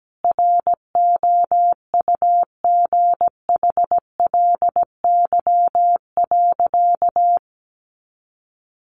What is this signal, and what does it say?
Morse code, heard as a keyed tone, about 17 wpm, 700 Hz. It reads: ROUGHLY.
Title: Sentences from Top 1500 Words 17wpm